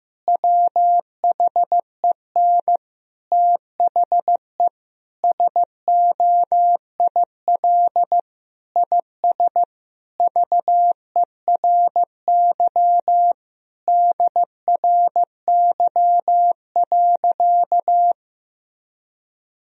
Morse code, heard as a keyed tone, about 15 wpm, 700 Hz. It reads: WHEN THE SOIL IS VERY DRY.
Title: Sentences from Top 1000 Words - Review Part 4 - 15wpm